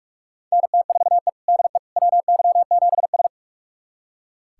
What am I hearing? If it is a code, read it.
Morse code, heard as a keyed tone, about 46 wpm, 700 Hz. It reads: NT4E DE WY7S
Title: USA Callsign DE Callsign - 4 Characters - 45wpm